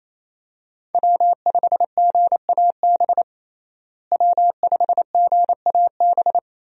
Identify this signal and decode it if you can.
Morse code, heard as a keyed tone, about 27 wpm, 700 Hz. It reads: W5GAB W5GAB
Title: USA Call Signs - 5 Characters - Repeated 2x - 28wpm